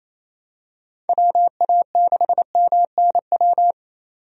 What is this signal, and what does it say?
Morse code, heard as a keyed tone, about 28 wpm, 700 Hz. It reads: WA6MNW
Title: USA Call Signs - 6 Characters - 28wpm